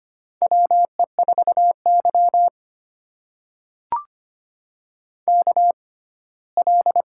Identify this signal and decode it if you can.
Morse code, heard as a keyed tone, about 25 wpm, spaced as for 17 wpm, 700 Hz. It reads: WE4Y KL
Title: USA Call Signs - 6x Character Spacing - 25wpm